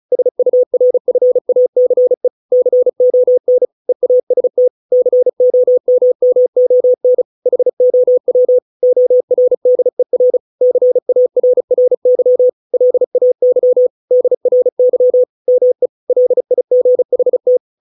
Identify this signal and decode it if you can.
Morse code, heard as a keyed tone, about 34 wpm, 500 Hz. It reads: SURFACE CON EAST COMMON HOW ORDER CARRY LAY DRY ME LIGHT